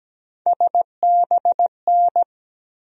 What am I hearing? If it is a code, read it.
Morse code, heard as a keyed tone, about 17 wpm, 700 Hz. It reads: SBN
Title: USA Call Sign Postfixes - 3 Letters 17wpm